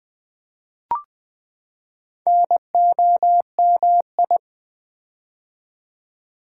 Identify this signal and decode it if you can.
Morse code, heard as a keyed tone, about 20 wpm, 700 Hz. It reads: NOMI
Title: Build A Word - 4 Letters At A Time - 10-Letter Words - 20wpm